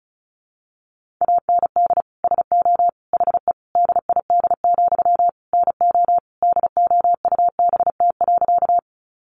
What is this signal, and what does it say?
Morse code, heard as a keyed tone, about 34 wpm, 700 Hz. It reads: AND SO HE DID, NO DOUBT.
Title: Sentences from Top 1500 Words 35wpm